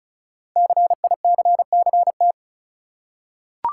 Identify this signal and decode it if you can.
Morse code, heard as a keyed tone, about 36 wpm, 690 Hz. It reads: CICCT
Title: Code Groups - 5 Letters - 35wpm